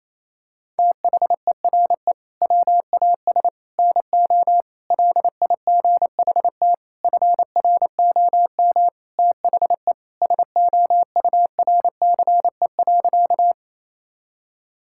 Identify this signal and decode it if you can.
Morse code, heard as a keyed tone, about 28 wpm, 700 Hz. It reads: THERE WAS NO LIGHT FROM THE SOURCE.
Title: Sentences from Top 1000 Words - Review Part 5 - 28wpm